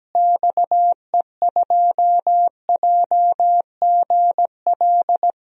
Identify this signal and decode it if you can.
Morse code, heard as a keyed tone, about 17 wpm, 700 Hz. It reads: XE2JGL
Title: DXCC-100 Call Signs - 6 Characters - Repeated 3x - 17wpm